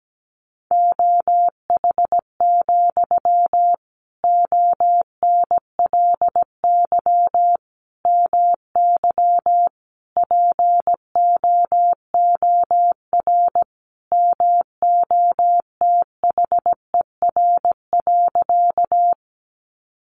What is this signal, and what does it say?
Morse code, heard as a keyed tone, about 17 wpm, 700 Hz. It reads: OH, ONLY MY POOR MOTHER.
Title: Sentences from Top 1000 Words 17wpm